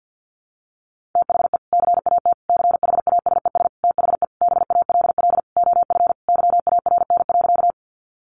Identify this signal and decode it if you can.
Morse code, heard as a keyed tone, about 50 wpm, 700 Hz. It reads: THE CAT CHASES THE BALL OF YARN.
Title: Additional Easy Phrases 50wpm